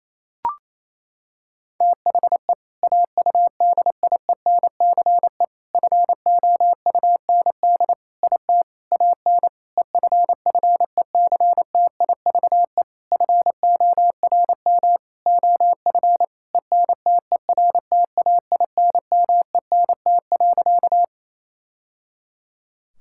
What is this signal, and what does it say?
Morse code, heard as a keyed tone, about 28 wpm, 700 Hz. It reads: THE AUDIENCE FOUND IT AN EFFECTIVE FORM OF ENTERTAINMENT.
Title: Sentences from Top 1000 Words - Review Part 6 - 28wpm